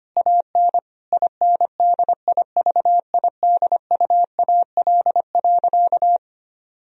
Morse code, heard as a keyed tone, about 25 wpm, 700 Hz. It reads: AN INDIVIDUAL.
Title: Sentences from Top 1000 Words - Review Part 4 - 25wpm